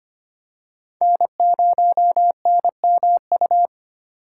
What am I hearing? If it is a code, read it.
Morse code, heard as a keyed tone, about 25 wpm, 700 Hz. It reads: N0NMU
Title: USA Call Signs - 3x Character Spacing - 25wpm